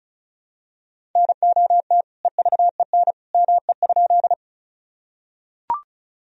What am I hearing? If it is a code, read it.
Morse code, heard as a keyed tone, about 36 wpm, 700 Hz. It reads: NOT EVEN ME?
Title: Sentences from Top 1000 Words - Review Part 1 - 35wpm